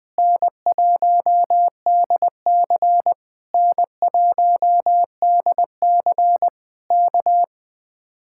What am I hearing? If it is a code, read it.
Morse code, heard as a keyed tone, about 20 wpm, 700 Hz. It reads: N1DC N1DC K